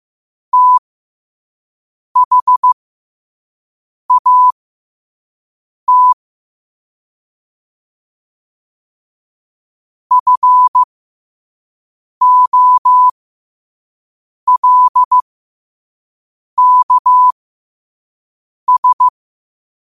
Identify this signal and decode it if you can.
Morse code, heard as a keyed tone, about 15 wpm, spaced as for 5 wpm, 1000 Hz. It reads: THAT FOLKS
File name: qotdfiles / 5wpm